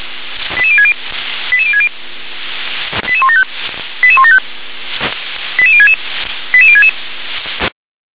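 Five-tone selective call (ZVEI), heard as quick runs of five tones, tones of about 70 ms. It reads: 88060 88060 88175 88175 88066 88066
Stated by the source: Received on SDR, ZVEI five-tone selective call.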